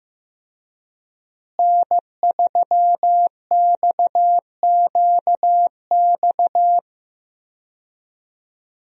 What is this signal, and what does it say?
Morse code, heard as a keyed tone, about 15 wpm, 700 Hz. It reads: N3XQX
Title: USA Call Signs - 5 Characters - Repeated 2x - 15wpm